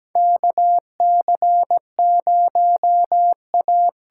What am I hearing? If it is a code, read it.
Morse code, heard as a keyed tone, about 17 wpm, 700 Hz. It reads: KC0A
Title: USA Call Signs - 6 Characters - 8x Character Spacing - 17wpm